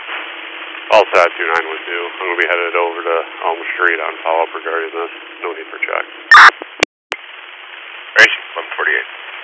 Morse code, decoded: none

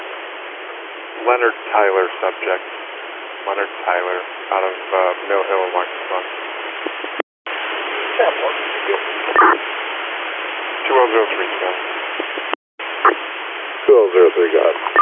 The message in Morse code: none

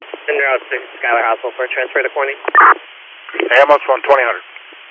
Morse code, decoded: none

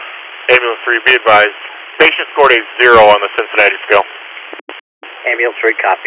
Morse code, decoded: none

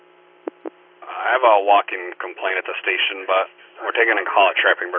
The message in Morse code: none

